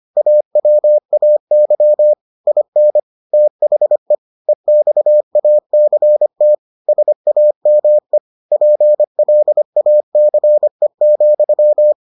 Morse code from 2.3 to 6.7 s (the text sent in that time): IN THE EXACT